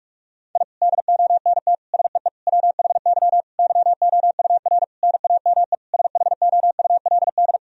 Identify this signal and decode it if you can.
Morse code, heard as a keyed tone, about 41 wpm, 700 Hz. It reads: I DONT SEE WHY YOUR NAME SHOULD
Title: Sentences from Top 1000 Words - Review Part 2 - 45wpm